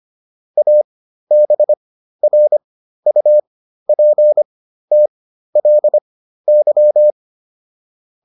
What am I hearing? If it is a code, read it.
Morse code, heard as a keyed tone, about 25 wpm, spaced as for 13 wpm, 600 Hz. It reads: ABRUPTLY